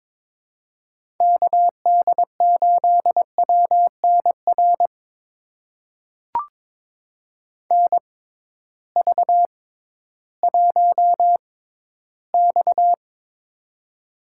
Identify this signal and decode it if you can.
Morse code, heard as a keyed tone, about 22 wpm, spaced as for 13 wpm, 700 Hz. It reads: KD8WNR NV1X
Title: USA Call Signs - 6x Character Spacing - 22wpm